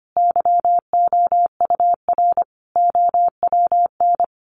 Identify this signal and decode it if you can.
Morse code, heard as a keyed tone, about 25 wpm, 700 Hz. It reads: YOUR OWN
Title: Sentences from Top 1500 Words 25wpm